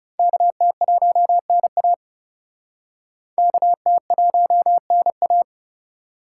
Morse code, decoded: KT1NA KT1NA

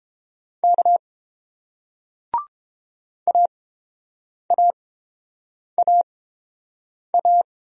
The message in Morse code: K AAAA

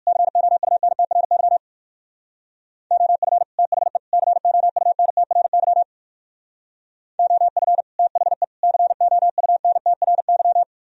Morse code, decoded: COUNTRY OF THE COUNTRY OF THE COUNTRY

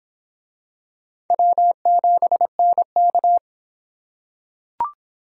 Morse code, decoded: W7NK

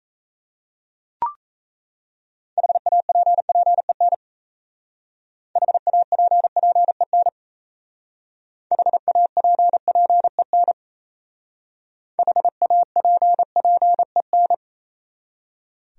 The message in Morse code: HAPPEN HAPPEN HAPPEN HAPPEN

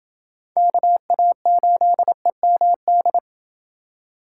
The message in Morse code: KA8EMD

